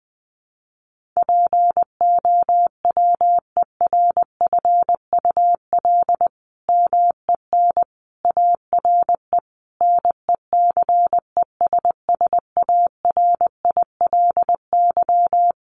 Morse code, heard as a keyed tone, about 20 wpm, 700 Hz. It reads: POWERFUL MEN ARE NECESSARILY
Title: Sentences from Top 1400 Words 20wpm